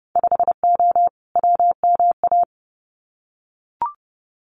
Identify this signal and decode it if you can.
Morse code, heard as a keyed tone, about 30 wpm, 700 Hz. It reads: 5O WMA
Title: Winter Field Day Exchanges 30wpm